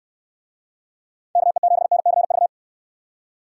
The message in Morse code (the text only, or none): KE7NCV